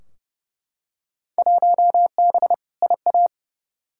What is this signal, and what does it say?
Morse code, heard as a keyed tone, about 30 wpm, 700 Hz. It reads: 1B IA
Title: Field Day Exchanges Encore - 30wpm